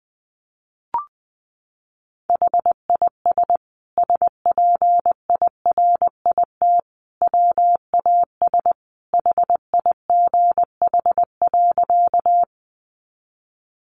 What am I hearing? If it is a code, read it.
Morse code, heard as a keyed tone, about 20 wpm, 700 Hz. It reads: HIS SPIRIT WAS HIGH.
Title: Sentences from Top 1400 Words 20wpm